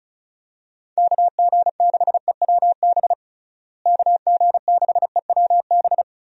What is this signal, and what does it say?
Morse code, heard as a keyed tone, about 34 wpm, 700 Hz. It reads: KG6EWB KG6EWB
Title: USA Call Signs - 6 Characters - Repeated 3x - 35wpm